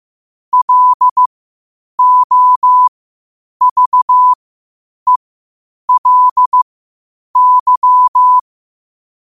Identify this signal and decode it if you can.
Morse code, heard as a keyed tone, about 15 wpm, spaced as for 8 wpm, 1000 Hz. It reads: LOVELY
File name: qotdfiles / 8wpm